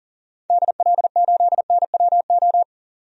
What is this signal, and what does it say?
Morse code, heard as a keyed tone, about 40 wpm, 700 Hz. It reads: DL8NWO
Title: DXCC-100 Call Signs - 6 Characters - 40wpm